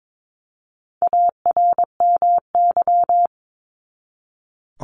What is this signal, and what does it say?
Morse code, heard as a keyed tone, about 22 wpm, 700 Hz. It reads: ARMY
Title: Top 501-750 Words 22wpm